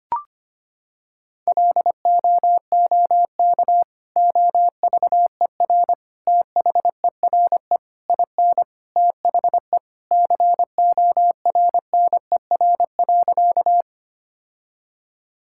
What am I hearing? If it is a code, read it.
Morse code, heard as a keyed tone, about 25 wpm, 700 Hz. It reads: LOOK OVER THERE IN THE CORNER.
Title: Sentences from Top 1600 Words 25wpm